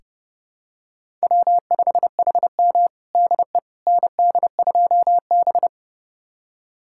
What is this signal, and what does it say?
Morse code, heard as a keyed tone, about 29 wpm, 700 Hz. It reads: W5HM DE ND2B
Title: USA Callsign DE Callsign - 4 Characters - 30wpm